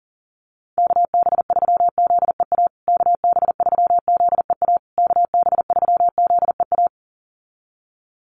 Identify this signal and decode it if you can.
Morse code, heard as a keyed tone, about 37 wpm, 700 Hz. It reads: KB3ZEA KB3ZEA KB3ZEA
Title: USA Call Signs Repeated 3x 40wpm